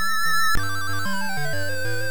There is no code.